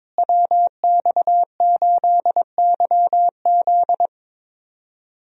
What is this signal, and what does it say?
Morse code, heard as a keyed tone, about 22 wpm, 700 Hz. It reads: WX8YZ
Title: USA Call Signs - 5 Characters - Repeated 2x - 22wpm